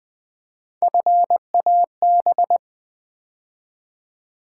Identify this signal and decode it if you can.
Morse code, heard as a keyed tone, about 20 wpm, 700 Hz. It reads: FAB
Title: Build A Word - 3 Letters At A Time - 6-Letter Words - 20wpm